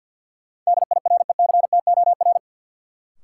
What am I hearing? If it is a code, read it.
Morse code, heard as a keyed tone, about 47 wpm, 700 Hz. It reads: DIRECTOR
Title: Sentences from Top 1300 Words - New Word Drill - 50wpm